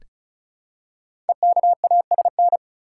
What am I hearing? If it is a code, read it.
Morse code, heard as a keyed tone, about 36 wpm, 700 Hz. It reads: EKASN
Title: Code Groups - 5 Letters - 35wpm